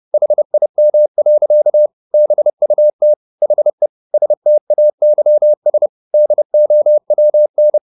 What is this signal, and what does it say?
Morse code, heard as a keyed tone, about 30 wpm, 600 Hz. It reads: HIM. BUT HE STAYS DOWN
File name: the old man and the sea / the old man and the sea 1 30wpm